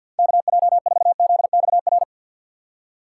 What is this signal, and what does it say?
Morse code, heard as a keyed tone, about 49 wpm, 700 Hz. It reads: KJ4ZXR
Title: USA Call Signs - 6 Characters - 4x Character Spacing - 50wpm